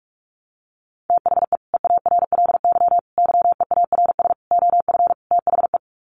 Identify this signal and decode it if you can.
Morse code, heard as a keyed tone, about 44 wpm, 700 Hz. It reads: THE EARLY YEARS OF THE
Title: Sets of 5 Words Encore 45wpm